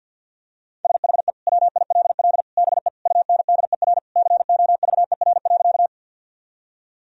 Morse code, heard as a keyed tone, about 53 wpm, 700 Hz. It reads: SHE WILL BE UNDER COVER.
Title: Sentences from Top 1000 Words - Review Part 1 - 50wpm